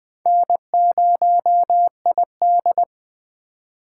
Morse code, heard as a keyed tone, about 20 wpm, 700 Hz. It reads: N0ID
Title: USA Call Signs - 4 Characters - 2x Character Spacing - 20wpm